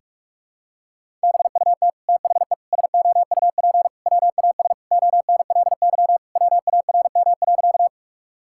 Morse code, heard as a keyed tone, about 46 wpm, 700 Hz. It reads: BUT THE SOUP WAS ONLY WARM.
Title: Sentences from Top 1000 Words - Review Part 6 - 45wpm